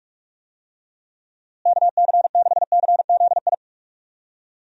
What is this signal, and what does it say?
Morse code, heard as a keyed tone, about 47 wpm, 700 Hz. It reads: KC6CZI
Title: USA Call Signs - 6 Characters - Repeated 2x - 45wpm